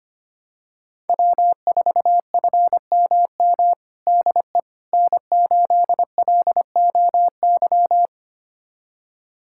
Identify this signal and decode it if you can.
Morse code, heard as a keyed tone, about 25 wpm, 700 Hz. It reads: W4FMM DE N8LOY
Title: USA Callsign DE Callsign - 5 Characters - 25wpm